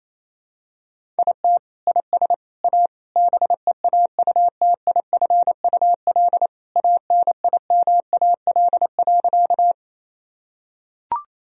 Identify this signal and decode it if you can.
Morse code, heard as a keyed tone, about 28 wpm, 700 Hz. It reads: IT IS A BEAUTIFUL ANIMAL.
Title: Sentences from Top 1000 Words - Review Part 1 - 28wpm